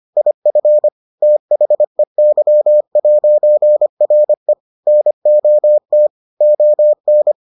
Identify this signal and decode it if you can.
Morse code, heard as a keyed tone, about 25 wpm, 600 Hz. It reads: IF THEY'RE NOT ON